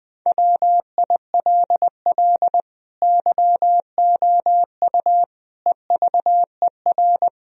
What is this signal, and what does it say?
Morse code, heard as a keyed tone, about 20 wpm, 700 Hz. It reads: WILL YOU EVER
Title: Sentences from Top 1000 Words - Review Part 3 - 20wpm